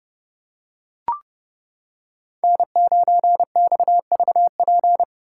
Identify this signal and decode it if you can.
Morse code, heard as a keyed tone, about 30 wpm, 700 Hz. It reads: N9XVP